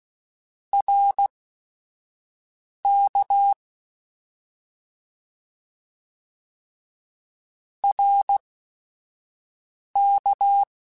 Morse code, heard as a keyed tone, about 16 wpm, spaced as for 5 wpm, 800 Hz. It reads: RK RK
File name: Short Runs / Short Runs-mp3s / ShortContrasts @5